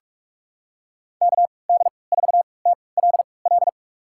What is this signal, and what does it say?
Morse code, heard as a keyed tone, about 46 wpm, spaced as for 26 wpm, 700 Hz. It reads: KD4TLL
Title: USA Call Signs - 3x Character Spacing - 45wpm